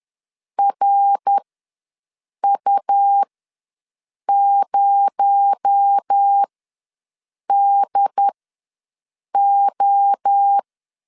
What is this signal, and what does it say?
Morse code, heard as a keyed tone, about 10 wpm, spaced as for 6 wpm, 800 Hz. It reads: RU0DO
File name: LEZIONI / 7 GF9Z / 7 G ws 10wpm 3843